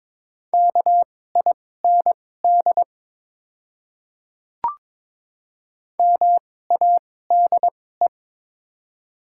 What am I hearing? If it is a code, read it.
Morse code, heard as a keyed tone, about 22 wpm, spaced as for 16 wpm, 690 Hz. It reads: KIND MADE
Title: 2x Character Spacing - Top 300 Words - 22wpm